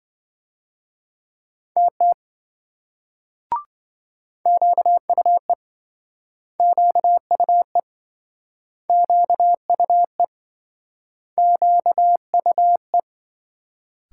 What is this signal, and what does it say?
Morse code, heard as a keyed tone, about 28 wpm, 700 Hz. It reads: TT QUE QUE QUE QUE